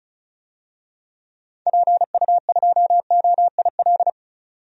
Morse code, heard as a keyed tone, about 36 wpm, 700 Hz. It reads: PU2OIL
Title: DXCC-100 Call Signs - 6 Characters - 35wpm